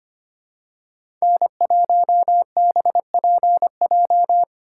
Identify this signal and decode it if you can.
Morse code, heard as a keyed tone, about 25 wpm, 700 Hz. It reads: N1BPJ
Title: USA Call Signs - 4x Character Spacing - 25wpm